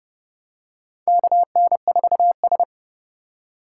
Morse code, heard as a keyed tone, about 30 wpm, 700 Hz. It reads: KN4S
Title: USA Call Signs - 4 Characters - 3x Character Spacing - 30wpm